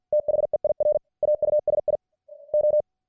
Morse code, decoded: THEIR AUSI O